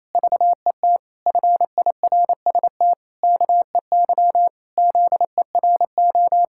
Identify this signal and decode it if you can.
Morse code, heard as a keyed tone, about 28 wpm, 700 Hz. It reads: VET FIRST KEY ZERO
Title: Pangram Challenge v2 - 28wpm